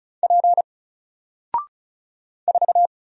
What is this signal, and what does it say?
Morse code, heard as a keyed tone, about 35 wpm, 700 Hz. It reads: P 4
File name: Single-Letter-Number-35wpm